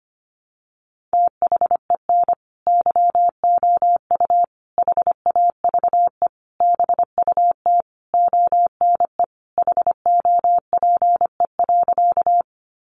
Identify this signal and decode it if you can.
Morse code, heard as a keyed tone, about 25 wpm, 700 Hz. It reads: THEN YOU HAVE BUT ONE HOPE.
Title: Sentences from Top 1000 Words 25wpm